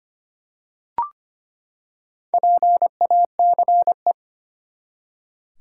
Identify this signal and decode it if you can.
Morse code, heard as a keyed tone, about 25 wpm, 700 Hz. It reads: PACE
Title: Four Letter Words - 25wpm